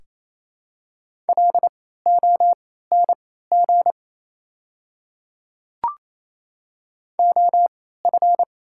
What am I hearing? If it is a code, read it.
Morse code, heard as a keyed tone, about 28 wpm, spaced as for 16 wpm, 700 Hz. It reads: LONG OF
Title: 3x Character Spacing - Top 100 Words - 28wpm